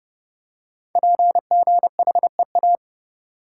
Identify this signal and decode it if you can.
Morse code, heard as a keyed tone, about 31 wpm, 700 Hz. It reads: PGHEA
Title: Code Groups - 5 Letters - 30wpm